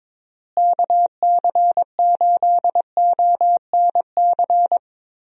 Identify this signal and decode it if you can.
Morse code, heard as a keyed tone, about 22 wpm, 700 Hz. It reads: KC8ONC
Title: USA Call Signs - 6 Characters - Repeated 3x - 22wpm